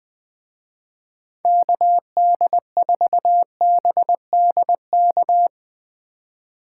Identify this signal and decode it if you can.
Morse code, heard as a keyed tone, about 19 wpm, 700 Hz. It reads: KD4BDK
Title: Speed Racing - USA 2x3 Call Signs 17wpm